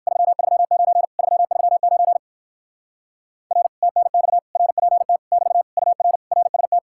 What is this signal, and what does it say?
Morse code, heard as a keyed tone, about 56 wpm, 700 Hz. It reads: 339 339 R TNX RPT <BT> UR RST
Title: Speed-Racing Common QSO Phrases - 45wpm